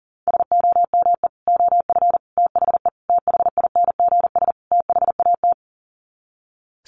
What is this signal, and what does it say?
Morse code, heard as a keyed tone, about 40 wpm, 700 Hz. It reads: SOME OF THE THINGS THAT
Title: Sets of 5 Words 40wpm